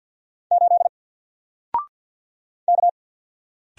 Morse code, heard as a keyed tone, about 48 wpm, 700 Hz. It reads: 8 K
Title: Single Letter-Number Alt 50wpm